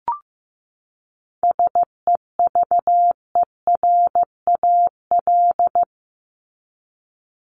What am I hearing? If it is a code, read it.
Morse code, heard as a keyed tone, about 15 wpm, 700 Hz. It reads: SEVERAL
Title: Top 500 Words 15wpm